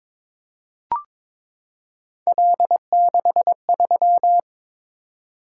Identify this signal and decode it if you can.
Morse code, heard as a keyed tone, about 22 wpm, 700 Hz. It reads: L63
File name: practice / 22wpm / Three Letter-Number Combinations - 22wpm